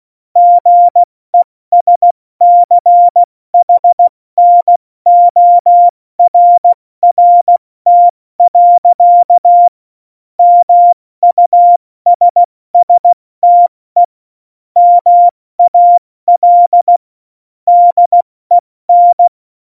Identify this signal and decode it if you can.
Morse code, heard as a keyed tone, about 16 wpm, 700 Hz. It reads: GESCHNORRT. MUSSTE MAL DEN
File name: Morsekurs / Witze / Witz 08. Dezember 2020, 70 F80